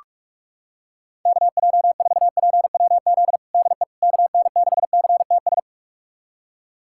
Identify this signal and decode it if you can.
Morse code, heard as a keyed tone, about 42 wpm, 700 Hz. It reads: KJ4PWZ DE KN6CTS
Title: USA Callsign DE Callsign 45wpm